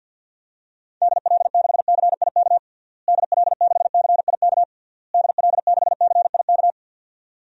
Morse code, heard as a keyed tone, about 52 wpm, 700 Hz. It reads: DL6CIK DL6CIK DL6CIK
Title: DXCC-100 Call Signs - 6 Characters - Repeated 3x - 50wpm